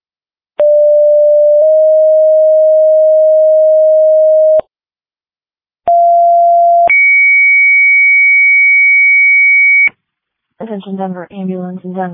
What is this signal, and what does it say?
Two-tone page, heard as a one-second tone, then a three-second tone, about 590 then 620 Hz, about 690 then 2250 Hz, with speech. It is heard from 0.6 to 9.9 s.